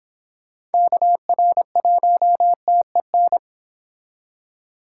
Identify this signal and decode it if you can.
Morse code, heard as a keyed tone, about 30 wpm, 700 Hz. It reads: KR1TEN